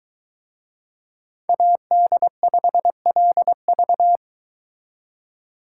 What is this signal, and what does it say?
Morse code, heard as a keyed tone, about 23 wpm, 700 Hz. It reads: AD5LV